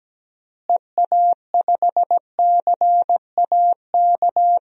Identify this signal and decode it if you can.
Morse code, heard as a keyed tone, about 17 wpm, 700 Hz. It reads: EA5CAK
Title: DXCC-100 Call Signs - 6 Characters - Repeated 3x - 17wpm